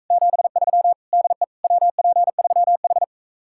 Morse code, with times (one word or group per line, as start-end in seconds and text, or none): 0.1-0.9 73
1.1-1.4 DE
1.6-3.0 WP3H